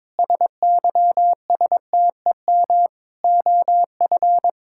Morse code, heard as a keyed tone, about 22 wpm, 700 Hz. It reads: SYSTEM OF
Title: Sentences from Top 1000 Words - Review Part 6 - 22wpm